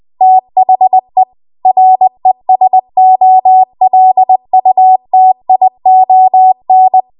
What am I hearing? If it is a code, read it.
Morse code, heard as a keyed tone, about 20 wpm, 750 Hz. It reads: THE RESOLUTION
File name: Morse / 20WPM